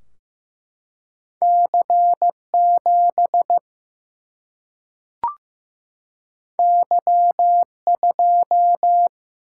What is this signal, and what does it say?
Morse code, heard as a keyed tone, about 15 wpm, 700 Hz. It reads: C 7 Y 2
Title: Single-Letter Single-Number 15wpm